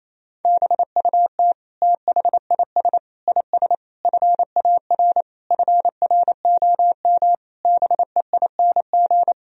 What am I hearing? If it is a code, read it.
Morse code, heard as a keyed tone, about 28 wpm, 700 Hz. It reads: BUT THIS IS FAR FROM BEING
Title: Sentences from Top 1000 Words - Review Part 3 - 28wpm